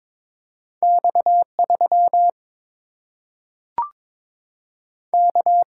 Morse code, heard as a keyed tone, about 21 wpm, 700 Hz. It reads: X 3 K
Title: Single-Letter Single-Number 22wpm